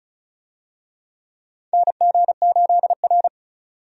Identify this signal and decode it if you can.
Morse code, heard as a keyed tone, about 36 wpm, 700 Hz. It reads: NG8R
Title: USA Call Signs - 4 Characters - 35wpm